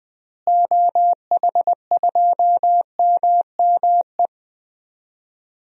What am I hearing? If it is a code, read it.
Morse code, heard as a keyed tone, about 20 wpm, 700 Hz. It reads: OH2MME
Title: DXCC-100 Call Signs - 6 Characters - Repeated 3x - 20wpm